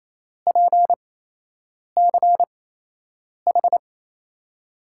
Morse code, heard as a keyed tone, about 27 wpm, spaced as for 8 wpm, 700 Hz. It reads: PCH
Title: USA Call Signs - 6 Characters - 8x Character Spacing - 28wpm